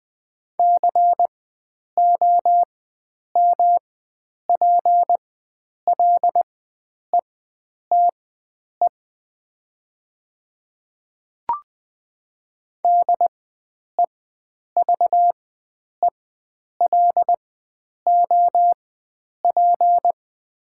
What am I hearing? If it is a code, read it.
Morse code, heard as a keyed tone, about 21 wpm, spaced as for 9 wpm, 700 Hz. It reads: COMPLETE DEVELOP
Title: 4x Character Spacing - Top 500 Words - 20wpm